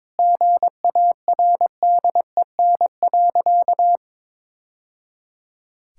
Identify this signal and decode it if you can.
Morse code, heard as a keyed tone, about 22 wpm, 700 Hz. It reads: GARDEN.
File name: practice / 22wpm / Sentences with 3-Word Phrases - 22wpm